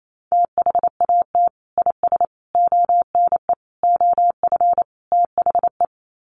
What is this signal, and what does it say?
Morse code, heard as a keyed tone, about 28 wpm, 700 Hz. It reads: THAT IS ONE OF THE
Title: Sets of 5 Words Encore 28wpm